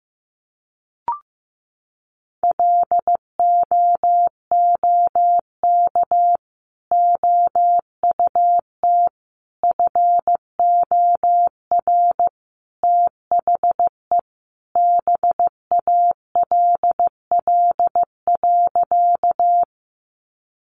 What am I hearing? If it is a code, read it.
Morse code, heard as a keyed tone, about 15 wpm, 700 Hz. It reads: LOOK OUT FOR THE BALL.
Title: Sentences from Top 900 Words 15wpm